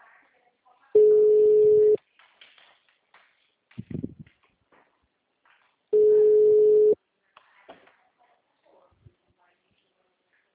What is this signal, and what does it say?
Ringback tone (Europe, 425 Hz), with speech.